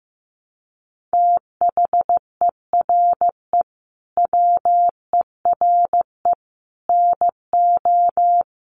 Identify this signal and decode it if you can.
Morse code, heard as a keyed tone, about 15 wpm, 700 Hz. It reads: THERE WERE NO
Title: Sets of 5 Words Encore2 15wpm